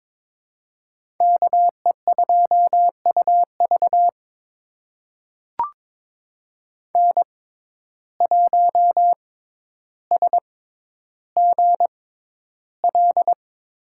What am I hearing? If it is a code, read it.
Morse code, heard as a keyed tone, about 22 wpm, spaced as for 11 wpm, 700 Hz. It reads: KE2UV N1SGL
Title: USA Call Signs - 5 Characters - 6x Character Spacing - 22wpm